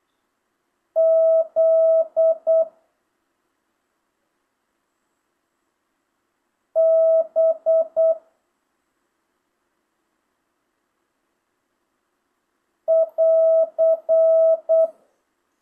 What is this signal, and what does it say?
Morse code, heard as a keyed tone, about 8 wpm, 650 Hz. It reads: Z B <AR>
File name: jh8cbh / 2025cw